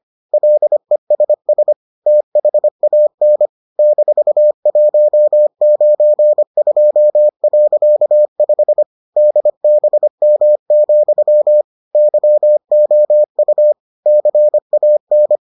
Morse code, heard as a keyed tone, about 25 wpm, 600 Hz. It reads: LESS THAN -192.5 DBM, YOU CAN